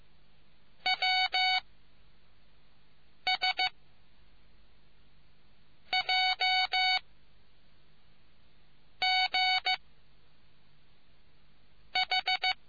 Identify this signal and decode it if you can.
Morse code, heard as a keyed tone, about 15 wpm, 750 Hz. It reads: W S J G H